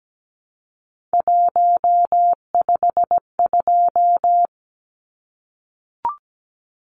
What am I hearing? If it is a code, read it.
Morse code, heard as a keyed tone, about 17 wpm, 700 Hz. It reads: 152
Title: Numbers - 3-Digit 17wpm